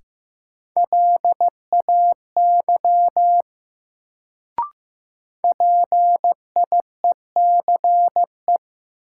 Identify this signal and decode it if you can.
Morse code, heard as a keyed tone, about 15 wpm, 700 Hz. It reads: LAY PIECE
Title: Sentences from Top 400 Words - New Word Drill - 15wpm